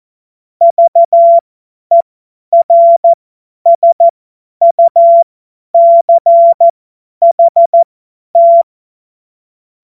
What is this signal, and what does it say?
Morse code, heard as a keyed tone, about 14 wpm, spaced as for 10 wpm, 680 Hz. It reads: VERSUCHT
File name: Morsekurs / Witze / Witz 11. Mai 2021, 50 F70